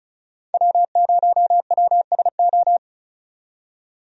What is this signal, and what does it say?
Morse code, heard as a keyed tone, about 36 wpm, 700 Hz. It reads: W0WSO
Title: USA Call Signs - 5 Characters - 35wpm